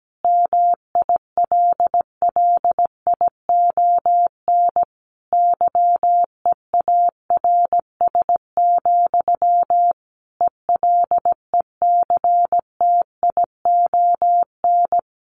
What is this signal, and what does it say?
Morse code, heard as a keyed tone, about 17 wpm, 700 Hz. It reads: MILLION YEARS, ELECTION
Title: A Sound Of Thunder 17wpm